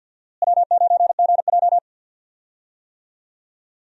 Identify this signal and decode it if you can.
Morse code, heard as a keyed tone, about 49 wpm, 700 Hz. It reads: W9GJ
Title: USA Call Signs - 4 Characters - Repeated 3x - 50wpm